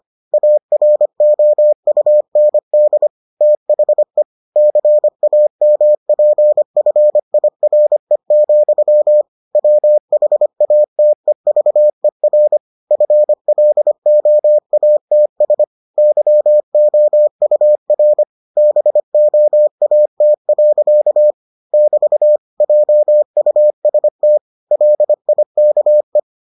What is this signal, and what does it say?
Morse code, heard as a keyed tone, about 25 wpm, 600 Hz. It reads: AROUND THE CAMPFIRE, WHATEVER FLOATS YOUR BOAT. <BT> JUST LIKE